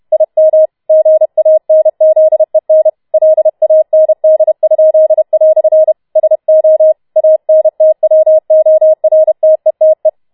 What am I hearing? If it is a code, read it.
Morse code, heard as a keyed tone, about 31 wpm, 630 Hz. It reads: IM GANZEN LAND?" SO ANTWORTETE